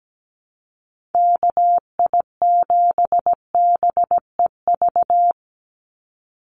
Morse code, heard as a keyed tone, about 17 wpm, 700 Hz. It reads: KI7BEV
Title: USA Call Signs 17wpm